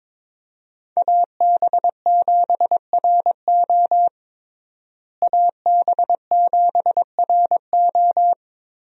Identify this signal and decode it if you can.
Morse code, heard as a keyed tone, about 22 wpm, 700 Hz. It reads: AB7RO AB7RO
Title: USA Call Signs - 5 Characters - Repeated 3x - 22wpm